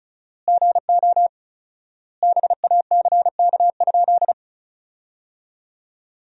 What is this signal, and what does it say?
Morse code, heard as a keyed tone, about 36 wpm, 700 Hz. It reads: GO BACK?
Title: 4x Word-Spacing - Sentences from Top 400 Words - 35wpm